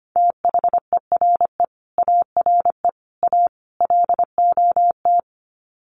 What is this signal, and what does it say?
Morse code, heard as a keyed tone, about 25 wpm, 700 Hz. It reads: THERE ARE A LOT
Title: Sets of 4 Words 25wpm